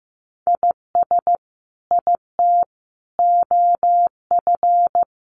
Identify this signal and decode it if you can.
Morse code, heard as a keyed tone, about 15 wpm, 700 Hz. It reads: IS IT OF
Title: Sentences from Top 1500 Words 15wpm